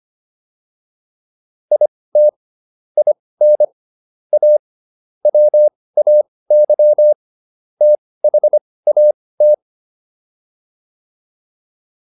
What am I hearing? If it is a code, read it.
Morse code, heard as a keyed tone, about 25 wpm, spaced as for 18 wpm, 600 Hz. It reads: IT IN A WAY THAT